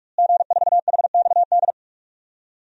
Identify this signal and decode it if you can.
Morse code, heard as a keyed tone, about 46 wpm, 700 Hz. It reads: G4HXD
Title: DXCC-100 Call Signs - 5 Characters - Repeated 2x - 45wpm